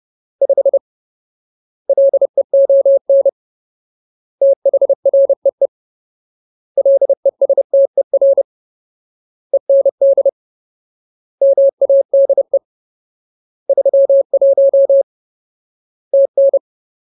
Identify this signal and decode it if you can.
Morse code, heard as a keyed tone, about 30 wpm, 550 Hz. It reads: 5 LEON THREE LESTER END MADE 31 TN
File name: Ultimate Word List / 30WPM Ultimate